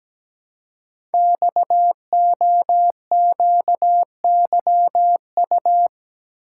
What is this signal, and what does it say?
Morse code, heard as a keyed tone, about 17 wpm, 700 Hz. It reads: XOQYU
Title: Code Groups - 5 Letters - 17wpm